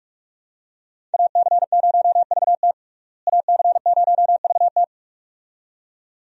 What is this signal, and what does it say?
Morse code, heard as a keyed tone, about 47 wpm, 700 Hz. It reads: AC0VT AC0VT
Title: USA Call Signs - 5 Characters - Repeated 2x - 45wpm